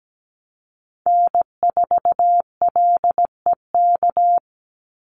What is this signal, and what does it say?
Morse code, heard as a keyed tone, about 17 wpm, 700 Hz. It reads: N4LEK
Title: USA Call Signs 17wpm